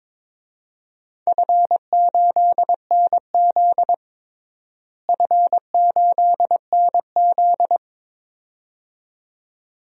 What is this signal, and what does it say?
Morse code, heard as a keyed tone, about 22 wpm, 700 Hz. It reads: F8NZ F8NZ
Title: DXCC-100 Call Signs - 4 Characters - Repeated 2x - 22wpm